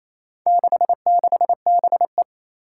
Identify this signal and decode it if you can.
Morse code, heard as a keyed tone, about 27 wpm, 700 Hz. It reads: 66BE
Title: Four Challenging Letter-Number Combinations 28wpm